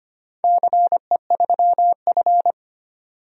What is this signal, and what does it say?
Morse code, heard as a keyed tone, about 25 wpm, 700 Hz. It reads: CE3F
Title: DXCC-100 Call Signs - 4 Characters - 25wpm